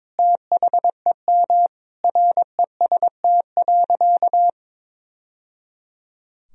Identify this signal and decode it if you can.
Morse code, heard as a keyed tone, about 21 wpm, 700 Hz. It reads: THEM REST.
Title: Sentences from Top 1000 Words - Review Part 4 - 22wpm